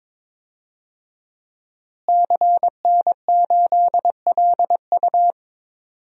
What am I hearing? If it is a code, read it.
Morse code, heard as a keyed tone, about 22 wpm, 700 Hz. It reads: CN8LU
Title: DXCC-100 Call Signs - 5 Characters - Repeated 2x - 22wpm